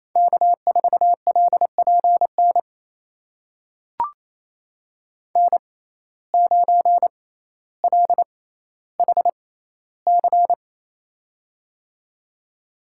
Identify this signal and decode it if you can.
Morse code, heard as a keyed tone, about 28 wpm, spaced as for 14 wpm, 700 Hz. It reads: K4LPN N9LHC
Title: USA Call Signs - 5 Characters - 6x Character Spacing - 28wpm